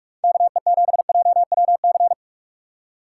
Morse code, heard as a keyed tone, about 54 wpm, 700 Hz. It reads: KE7JWC
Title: Speed Racing - USA 2x3 Call Signs 45wpm